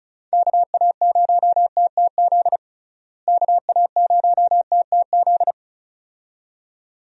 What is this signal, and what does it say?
Morse code, heard as a keyed tone, about 35 wpm, 700 Hz. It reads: KA0TTZ KA0TTZ
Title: USA Call Signs - 6 Characters - Repeated 2x - 35wpm